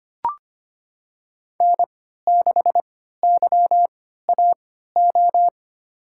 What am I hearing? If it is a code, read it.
Morse code, heard as a keyed tone, about 25 wpm, spaced as for 14 wpm, 700 Hz. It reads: N6YAO
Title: USA Call Signs - 3x Character Spacing - 25wpm